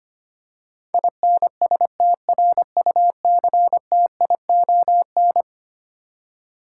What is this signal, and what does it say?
Morse code, heard as a keyed tone, about 25 wpm, 700 Hz. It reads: INSTRUCTION
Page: morse-code-ninja-practice-25wpm